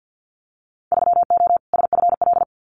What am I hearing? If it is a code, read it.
Morse code, heard as a keyed tone, about 50 wpm, 700 Hz. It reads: 3O SFL
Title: Winter Field Day Exchanges 50wpm